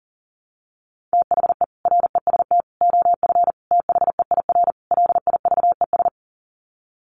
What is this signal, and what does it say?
Morse code, heard as a keyed tone, about 40 wpm, 700 Hz. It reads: THE REST OF THEIR LIVES
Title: Sets of 5 Words Encore 40wpm